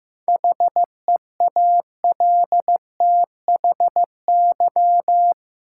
Morse code, heard as a keyed tone, about 15 wpm, 700 Hz. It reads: HEALTHY
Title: Sentences from Top 1400 Words - New Word Drill Alt - 15wpm